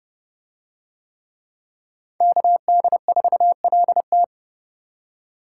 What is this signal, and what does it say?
Morse code, heard as a keyed tone, about 31 wpm, 700 Hz. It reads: KD4LT